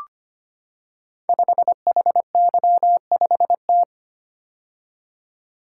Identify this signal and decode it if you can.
Morse code, heard as a keyed tone, about 25 wpm, 700 Hz. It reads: 5HY5T
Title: Code Groups - 5 Letter:Numbers - 25wpm